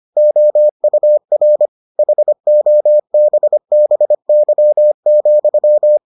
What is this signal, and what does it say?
Morse code, heard as a keyed tone, about 25 wpm, 600 Hz. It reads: OUR HOBBY,